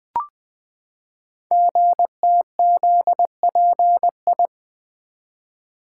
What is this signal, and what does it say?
Morse code, heard as a keyed tone, about 20 wpm, 700 Hz. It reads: GTZPI
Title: Code Groups - 5 Letters - 20wpm